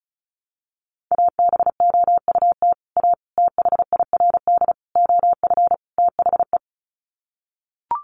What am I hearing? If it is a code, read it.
Morse code, heard as a keyed tone, about 34 wpm, 700 Hz. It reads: ABOUT A THIRD OF THE
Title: Sets of 5 Words Encore2 35wpm